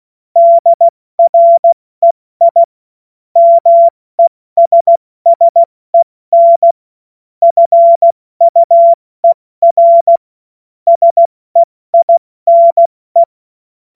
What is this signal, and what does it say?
Morse code, heard as a keyed tone, about 16 wpm, 680 Hz. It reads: DREI MESSEN FUER SEINE
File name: Morsekurs / Witze / Witz 17.Oktober 2023 70 F80